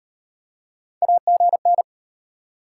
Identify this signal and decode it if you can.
Morse code, heard as a keyed tone, about 38 wpm, 700 Hz. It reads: AGN